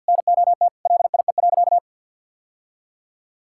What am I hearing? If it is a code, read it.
Morse code, heard as a keyed tone, about 50 wpm, 700 Hz. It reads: NOT LIE.
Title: Sentences from Top 1600 Words 50wpm